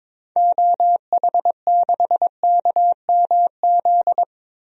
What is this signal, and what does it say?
Morse code, heard as a keyed tone, about 22 wpm, 700 Hz. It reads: OH6KMZ
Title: DXCC-100 Call Signs - 6 Characters - Repeated 3x - 22wpm